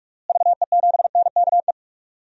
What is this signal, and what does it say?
Morse code, heard as a keyed tone, about 46 wpm, 700 Hz. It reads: VE7NKE
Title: DXCC-100 Call Signs - 6 Characters - Repeated 3x - 45wpm